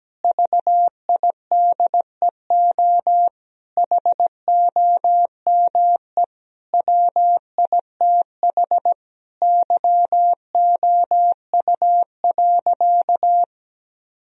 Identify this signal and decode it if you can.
Morse code, heard as a keyed tone, about 17 wpm, 700 Hz. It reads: VIDEO HOME WITH YOU.
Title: Sentences from Top 1000 Words - Review Part 4 - 17wpm